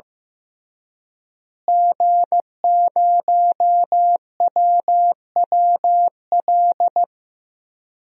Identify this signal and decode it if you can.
Morse code, heard as a keyed tone, about 16 wpm, 700 Hz. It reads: G0WWL
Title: DXCC-100 Call Signs - 5 Characters - Repeated 3x - 15wpm